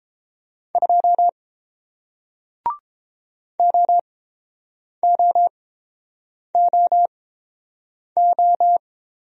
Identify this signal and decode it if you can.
Morse code, heard as a keyed tone, about 30 wpm, spaced as for 7 wpm, 690 Hz. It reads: 2 OOOO